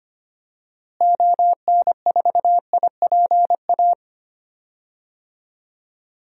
Morse code, heard as a keyed tone, about 25 wpm, 700 Hz. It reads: ON4IPA
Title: DXCC-100 Call Signs - 6 Characters - Repeated 3x - 25wpm